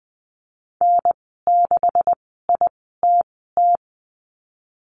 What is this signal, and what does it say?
Morse code, heard as a keyed tone, about 20 wpm, spaced as for 15 wpm, 700 Hz. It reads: N6ITT
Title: 128 Lesson 21g - Number 6 - Call Signs 20wpm